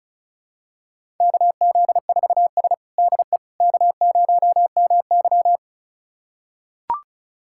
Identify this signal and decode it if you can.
Morse code, heard as a keyed tone, about 36 wpm, 700 Hz. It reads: KZ4S DE K0MY
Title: USA Callsign DE Callsign - 4 Characters - 35wpm